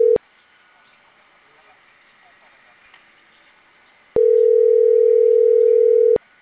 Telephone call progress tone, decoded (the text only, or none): ringback tone